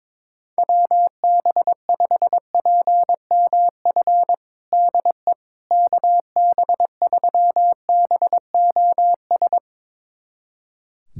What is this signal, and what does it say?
Morse code, heard as a keyed tone, about 22 wpm, 700 Hz. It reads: WB5PMF DE KB3BOS
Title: USA Callsign DE Callsign 22wpm